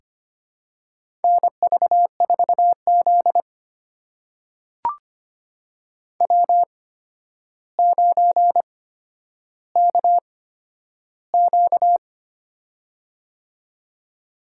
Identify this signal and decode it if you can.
Morse code, heard as a keyed tone, about 25 wpm, spaced as for 11 wpm, 700 Hz. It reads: NV4Z W9KQ
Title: USA Call Signs - 4 Characters - 8x Character Spacing - 25wpm